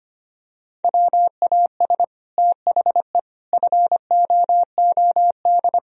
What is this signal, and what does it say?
Morse code, heard as a keyed tone, about 25 wpm, 700 Hz. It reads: WAS THE FOOD
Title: Sentences from Top 1000 Words - Review Part 1 - 25wpm